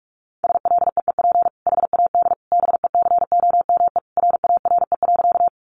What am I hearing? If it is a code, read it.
Morse code, heard as a keyed tone, about 44 wpm, 700 Hz. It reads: SLEEP HAD BECOME RARE.
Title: Sentences from Top 1100 Words 45wpm